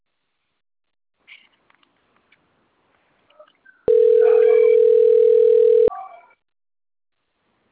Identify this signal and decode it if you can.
Ringback tone (North America), 440 + 480 Hz.